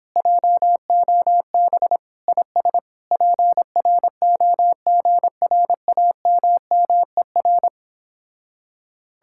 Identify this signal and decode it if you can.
Morse code, heard as a keyed tone, about 31 wpm, 700 Hz. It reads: JOB IS PROGRAMMER